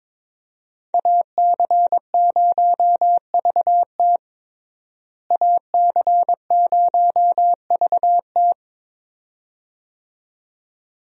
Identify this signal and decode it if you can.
Morse code, heard as a keyed tone, about 22 wpm, 700 Hz. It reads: AC0VT AC0VT
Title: USA Call Signs - 5 Characters - Repeated 2x - 22wpm